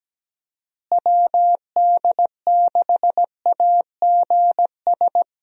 Morse code, heard as a keyed tone, about 17 wpm, 700 Hz. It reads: WD6AGS
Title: USA Call Signs - 6 Characters - Repeated 3x - 17wpm